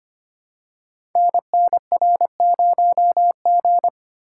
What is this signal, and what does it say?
Morse code, heard as a keyed tone, about 25 wpm, 700 Hz. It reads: NNR0G
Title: Code Groups - 5 Letter:Numbers - 25wpm